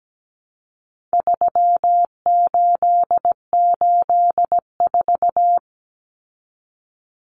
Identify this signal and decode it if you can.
Morse code, heard as a keyed tone, about 17 wpm, 700 Hz. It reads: 3884
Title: Common Cut Numbers - 4-Digit 17wpm